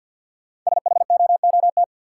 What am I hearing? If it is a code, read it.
Morse code, heard as a keyed tone, about 53 wpm, 700 Hz. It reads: SHOOT
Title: Sentences from Top 1200 Words - New Word Drill Alt - 50wpm